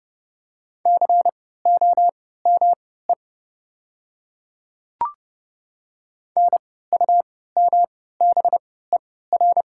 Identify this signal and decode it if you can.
Morse code, heard as a keyed tone, about 30 wpm, spaced as for 17 wpm, 700 Hz. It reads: COME NUMBER
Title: 3x Character Spacing - Top 100 Words - 30wpm